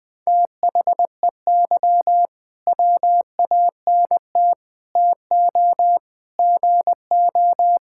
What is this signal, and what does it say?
Morse code, heard as a keyed tone, about 21 wpm, 700 Hz. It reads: THEY WANT TO GO
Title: Sentences with 5-Word Phrases - Spoken Beginning - 20wpm